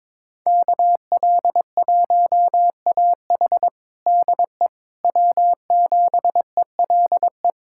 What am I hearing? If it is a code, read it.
Morse code, heard as a keyed tone, about 22 wpm, 700 Hz. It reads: KL1AH DE W7ELE
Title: USA Callsign DE Callsign 22wpm